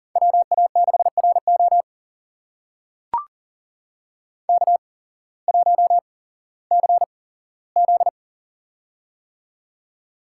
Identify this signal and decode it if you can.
Morse code, heard as a keyed tone, about 40 wpm, spaced as for 19 wpm, 690 Hz. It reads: WA6RO K1CZ
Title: USA Call Signs - 8x Character Spacing - 40wpm